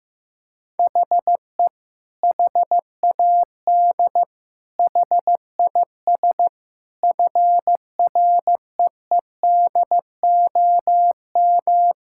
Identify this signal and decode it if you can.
Morse code, heard as a keyed tone, about 15 wpm, 700 Hz. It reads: HE HAD HIS FREEDOM
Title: Sentences from Top 1000 Words - Review Part 5 - 15wpm